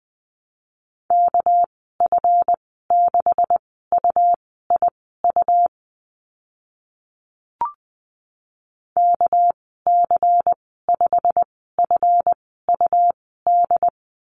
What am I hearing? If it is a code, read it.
Morse code, heard as a keyed tone, about 20 wpm, spaced as for 14 wpm, 700 Hz. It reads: KF6UIU KC5FUD
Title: 238 Lesson 38g - Letter q - Call Signs 20wpm